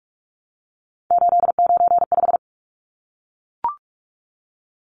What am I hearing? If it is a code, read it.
Morse code, heard as a keyed tone, about 44 wpm, 700 Hz. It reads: 895
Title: Numbers - 3-Digit 45wpm